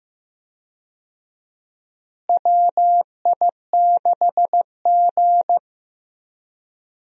Morse code, heard as a keyed tone, about 15 wpm, 700 Hz. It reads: WI6G
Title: USA Call Signs - 4 Characters - Repeated 3x - 15wpm